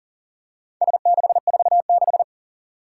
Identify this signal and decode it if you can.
Morse code, heard as a keyed tone, about 41 wpm, 700 Hz. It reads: S646
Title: Four Challenging Letter-Number Combinations 40wpm